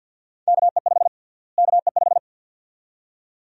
Morse code, heard as a keyed tone, about 52 wpm, 700 Hz. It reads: KE5 KE5
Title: USA Call Sign Prefixes - Repeated 2x - 50wpm